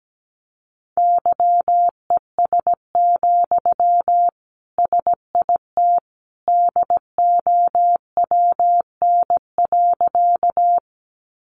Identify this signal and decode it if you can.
Morse code, heard as a keyed tone, about 17 wpm, 700 Hz. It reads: YES, SIT DOWN.
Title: Sentences from Top 500 Words 17wpm